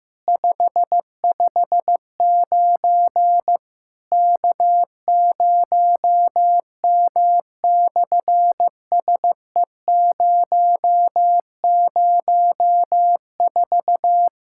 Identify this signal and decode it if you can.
Morse code, heard as a keyed tone, about 14 wpm, 700 Hz. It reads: 559 K0M/SE004
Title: SOTA QSO Practice 15wpm